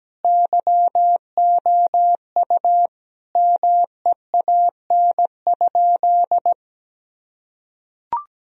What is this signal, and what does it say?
Morse code, heard as a keyed tone, about 17 wpm, 700 Hz. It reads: YOU MEAN?
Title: Sentences from Top 1000 Words - Review Part 3 - 17wpm